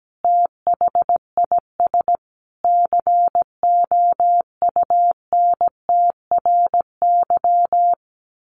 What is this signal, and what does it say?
Morse code, heard as a keyed tone, about 17 wpm, 700 Hz. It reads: THIS COUNTRY
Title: Sentences from Top 1200 Words 17wpm